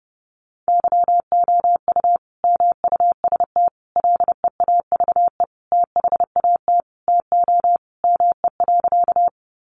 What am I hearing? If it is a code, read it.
Morse code, heard as a keyed tone, about 30 wpm, 700 Hz. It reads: YOU MUST LEAVE THAT TO ME.
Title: Sentences from Top 400 Words 30wpm